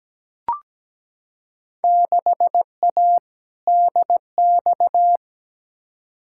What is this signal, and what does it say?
Morse code, heard as a keyed tone, about 17 wpm, 700 Hz. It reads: 6A DX
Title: Field Day Exchanges Encore - 17wpm